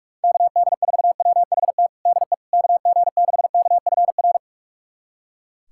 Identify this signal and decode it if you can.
Morse code, heard as a keyed tone, about 43 wpm, 700 Hz. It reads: KD4WHT DE KG6KFR
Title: USA Callsign DE Callsign - 6 Characters - 45wpm